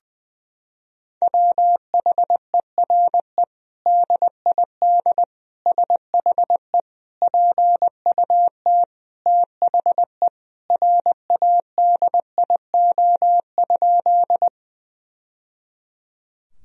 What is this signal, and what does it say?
Morse code, heard as a keyed tone, about 20 wpm, 700 Hz. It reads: WHERE DID SHE PUT THE RADIO?
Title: Sentences from Top 1000 Words - Review Part 3 - 20wpm